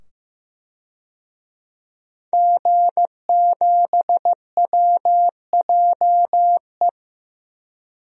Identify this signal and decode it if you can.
Morse code, heard as a keyed tone, about 15 wpm, 700 Hz. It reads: G7WJE
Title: DXCC-100 Call Signs - 5 Characters - 15wpm